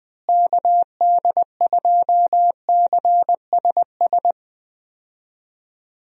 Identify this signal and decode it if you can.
Morse code, heard as a keyed tone, about 20 wpm, 700 Hz. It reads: KD2CSS
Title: USA Call Signs - 6 Characters - Repeated 2x - 20wpm